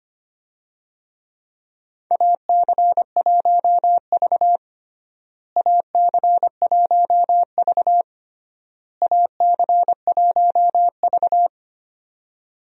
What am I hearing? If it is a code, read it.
Morse code, heard as a keyed tone, about 25 wpm, 700 Hz. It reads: AC1V AC1V AC1V
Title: USA Call Signs - 4 Characters - Repeated 3x - 25wpm